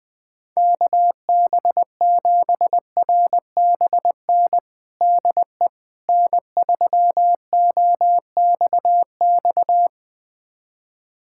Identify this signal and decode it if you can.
Morse code, heard as a keyed tone, about 20 wpm, 700 Hz. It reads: KB7RBN DE N3OXX
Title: USA Callsign DE Callsign 20wpm